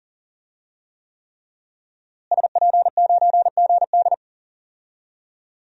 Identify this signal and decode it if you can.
Morse code, heard as a keyed tone, about 40 wpm, 700 Hz. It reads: SP9GD
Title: DXCC-100 Call Signs - 5 Characters - Repeated 3x - 40wpm